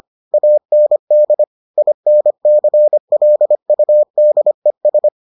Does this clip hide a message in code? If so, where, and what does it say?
Yes: Morse code from 0.3 to 5.1 s, heard as a keyed tone, about 25 wpm, 600 Hz. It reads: AND INCLUDES